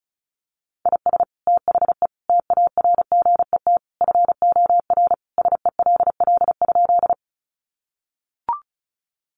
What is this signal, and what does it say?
Morse code, heard as a keyed tone, about 34 wpm, 700 Hz. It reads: IS THE TARGET FOR SELL?
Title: Sentences from Top 1200 Words 35wpm